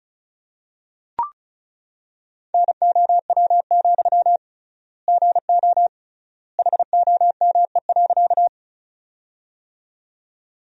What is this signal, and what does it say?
Morse code, heard as a keyed tone, about 36 wpm, 700 Hz. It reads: NOW, GO HOME.
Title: 3x Word-Spacing - Sentences from Top 200 Words - 35wpm